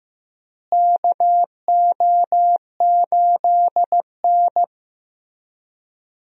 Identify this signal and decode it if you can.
Morse code, heard as a keyed tone, about 15 wpm, 700 Hz. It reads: KO8N
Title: USA Call Signs - 4 Characters - Repeated 3x - 15wpm